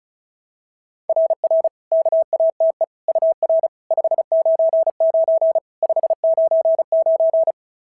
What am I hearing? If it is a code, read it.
Morse code, heard as a keyed tone, about 36 wpm, 660 Hz. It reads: RR KATE UR 599 599
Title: SOTA QSO Practice 35wpm